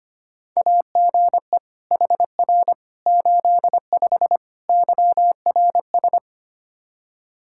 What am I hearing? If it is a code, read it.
Morse code, heard as a keyed tone, about 25 wpm, 700 Hz. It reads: AGE HR 85 YRS
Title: Practice QSO - Exchange 3 - More Info - 25wpm